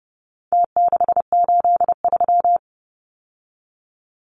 Morse code, heard as a keyed tone, about 30 wpm, 700 Hz. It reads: T683
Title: Numbers - Contest Style - 4-Digit 30wpm